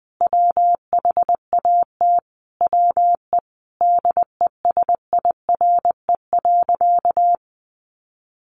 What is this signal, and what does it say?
Morse code, heard as a keyed tone, about 20 wpm, 700 Hz. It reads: WHAT WE DESIRE.
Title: Sentences from Top 1300 Words 20wpm